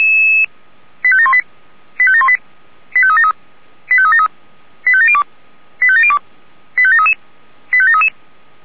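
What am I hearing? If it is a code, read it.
Five-tone selective call (ZVEI), heard as quick runs of five tones, tones of about 70 ms. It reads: 85718 85718 85373 85373 85702 85702 85733 85733
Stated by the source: Five-tone ZVEI selective call received on SDR.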